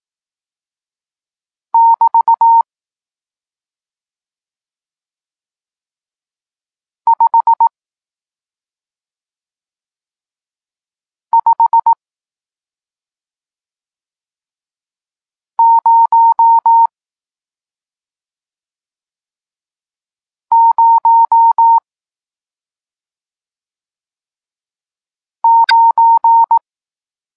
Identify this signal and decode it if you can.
Morse code, heard as a keyed tone, about 18 wpm, spaced as for 2 wpm, 920 Hz. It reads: <BT> 55009
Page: Tekst 40 – Nauka znaków przestankowych